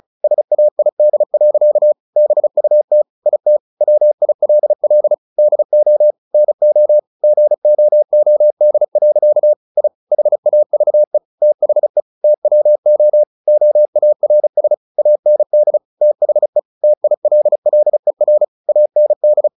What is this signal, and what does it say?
Morse code, heard as a keyed tone, about 34 wpm, 600 Hz. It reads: SAID. BUT IT WILL DO NO GOOD. I HAVE THE TWO OARS AND THE TILLER AND
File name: the old man and the sea 2 35wpm